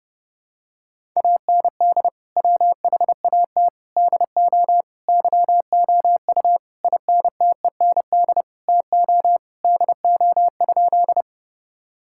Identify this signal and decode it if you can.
Morse code, heard as a keyed tone, about 32 wpm, 700 Hz. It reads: AND WHAT DO YOU INTEND TO DO?
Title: Sentences from Top 1600 Words 30wpm